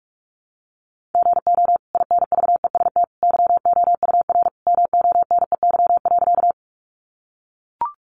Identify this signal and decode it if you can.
Morse code, heard as a keyed tone, about 44 wpm, 700 Hz. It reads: GO INVEST YOUR MONEY.
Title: Sentences from Top 1100 Words 45wpm